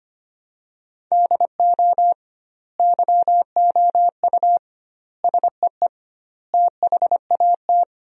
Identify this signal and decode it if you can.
Morse code, heard as a keyed tone, about 25 wpm, 700 Hz. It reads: DO YOU SEE THAT
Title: 2x Word-Spacing - Sentences from Top 500 Words - 25wpm